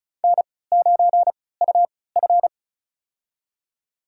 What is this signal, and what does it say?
Morse code, heard as a keyed tone, about 34 wpm, spaced as for 20 wpm, 700 Hz. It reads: N9UF
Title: USA Call Signs - 3x Character Spacing - 35wpm